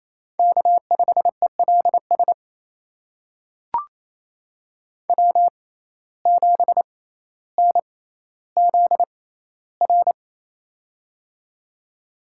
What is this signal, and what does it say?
Morse code, heard as a keyed tone, about 28 wpm, spaced as for 14 wpm, 700 Hz. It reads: K5ELS W7NZR
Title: USA Call Signs - 5 Characters - 6x Character Spacing - 28wpm